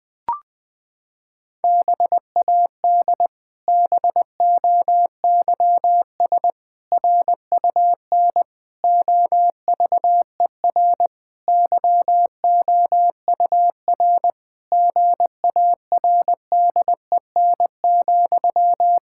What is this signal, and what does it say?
Morse code, heard as a keyed tone, about 20 wpm, 700 Hz. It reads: BAD BOYS RUN OVER YOUR GARDEN,
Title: Pangram Challenge v2 - 20wpm